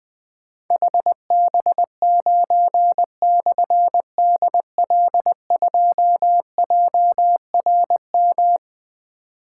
Morse code, heard as a keyed tone, about 20 wpm, 700 Hz. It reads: HB9/DL2JRM
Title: Difficult Call Signs 20wpm